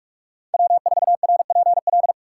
Morse code, heard as a keyed tone, about 44 wpm, 700 Hz. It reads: W4RPL